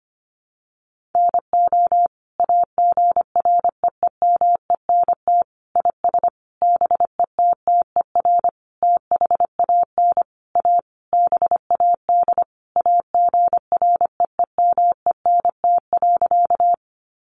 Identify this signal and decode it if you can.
Morse code, heard as a keyed tone, about 25 wpm, 700 Hz. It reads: NO AGREEMENT IS BETTER THAN A BAD AGREEMENT.
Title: Sentences from Top 1500 Words 25wpm